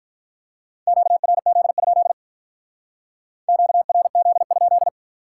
Morse code, heard as a keyed tone, about 45 wpm, 700 Hz. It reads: QRZ? QRZ?